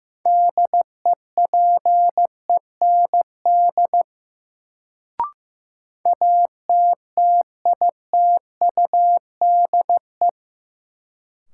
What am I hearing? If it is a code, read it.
Morse code, heard as a keyed tone, about 16 wpm, 700 Hz. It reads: DEPEND ATTITUDE
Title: Sentences from Top 1300 Words - New Word Drill Alt - 15wpm